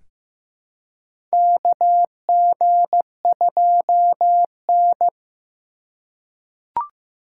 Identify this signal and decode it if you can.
Morse code, heard as a keyed tone, about 15 wpm, 700 Hz. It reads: KG2N
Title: USA Call Signs - 3x Character Spacing - 15wpm